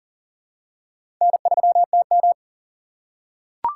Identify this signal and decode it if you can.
Morse code, heard as a keyed tone, about 40 wpm, 700 Hz. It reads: N3TM
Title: USA Call Signs - 4 Characters - 2x Character Spacing - 40wpm